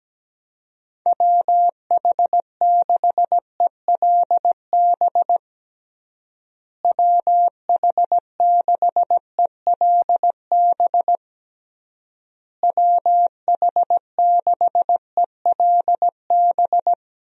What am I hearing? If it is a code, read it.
Morse code, heard as a keyed tone, about 17 wpm, 700 Hz. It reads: WH6ELB WH6ELB WH6ELB
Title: USA Call Signs - 6 Characters - Repeated 3x - 17wpm